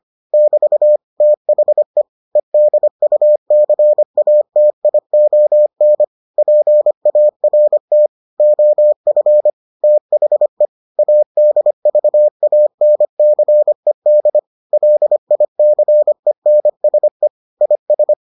Morse code, heard as a keyed tone, about 25 wpm, 600 Hz. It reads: <BT> THE EDUCATION PART OF THE ADVANCED LICENSE IS